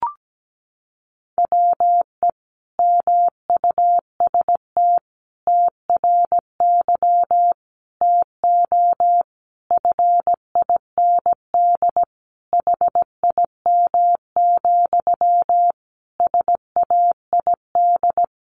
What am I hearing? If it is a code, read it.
Morse code, heard as a keyed tone, about 17 wpm, 700 Hz. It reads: WE MUST TRY TO FIND HIM, SAID
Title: Sentences from Top 400 Words 17wpm